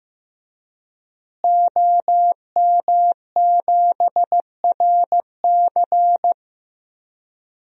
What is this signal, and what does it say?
Morse code, heard as a keyed tone, about 15 wpm, 700 Hz. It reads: OM7RC
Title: DXCC-100 Call Signs - 5 Characters - Repeated 2x - 15wpm